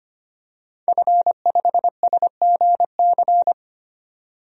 Morse code, heard as a keyed tone, about 25 wpm, 700 Hz. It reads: F5SGC
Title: DXCC-100 Call Signs - 5 Characters - 25wpm